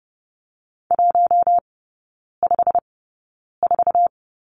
Morse code, heard as a keyed tone, about 30 wpm, spaced as for 9 wpm, 700 Hz. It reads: 154
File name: Lesson 15e - Number 5 - Three Characters 30wpm